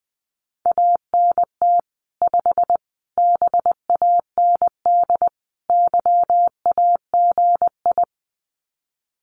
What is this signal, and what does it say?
Morse code, heard as a keyed tone, about 20 wpm, 700 Hz. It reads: ANT 5 BAND YAGI
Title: Common QSO Phrases 20wpm